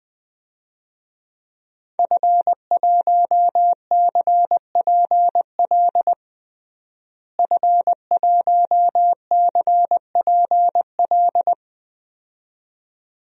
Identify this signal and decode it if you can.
Morse code, heard as a keyed tone, about 20 wpm, 700 Hz. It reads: F1CPL F1CPL
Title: DXCC-100 Call Signs - 5 Characters - Repeated 2x - 20wpm